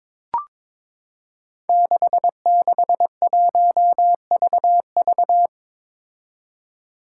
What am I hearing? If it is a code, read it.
Morse code, heard as a keyed tone, about 22 wpm, 700 Hz. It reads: 661VV
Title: Five Challenging Letter-Number Combinations 22wpm